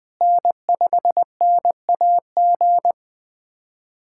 Morse code, heard as a keyed tone, about 20 wpm, 700 Hz. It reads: N5NAG
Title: USA Callsign DE Callsign 20wpm